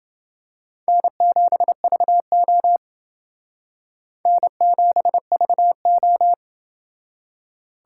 Morse code, heard as a keyed tone, about 29 wpm, 700 Hz. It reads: N7VO N7VO